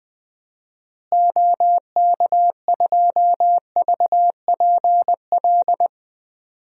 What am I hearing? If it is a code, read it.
Morse code, heard as a keyed tone, about 21 wpm, 700 Hz. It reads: OK2VPL
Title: DXCC-100 Call Signs - 6 Characters - Repeated 2x - 20wpm